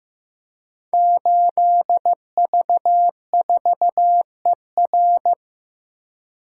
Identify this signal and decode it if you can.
Morse code, heard as a keyed tone, about 15 wpm, 700 Hz. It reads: 8V4ER
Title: Code Groups - 5 Letter:Numbers - 15wpm